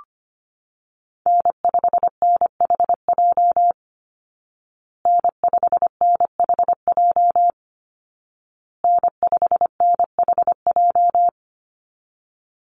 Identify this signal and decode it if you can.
Morse code, heard as a keyed tone, about 25 wpm, 700 Hz. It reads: N5NHJ N5NHJ N5NHJ
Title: Call Signs Repeated 2x - 25wpm